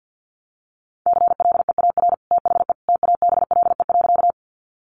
Morse code, heard as a keyed tone, about 50 wpm, 700 Hz. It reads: CLEAR THE TABLE.
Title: Sentences from Top 500 Words 50wpm